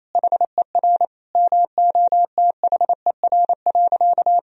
Morse code, heard as a keyed tone, about 28 wpm, 700 Hz. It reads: HER MOTHER.
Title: Sentences from Top 1000 Words - Review Part 4 - 28wpm